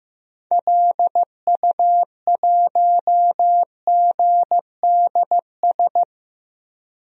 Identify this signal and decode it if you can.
Morse code, heard as a keyed tone, about 15 wpm, 700 Hz. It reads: LU1GDS
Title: DXCC-100 Call Signs - 6 Characters - Repeated 3x - 15wpm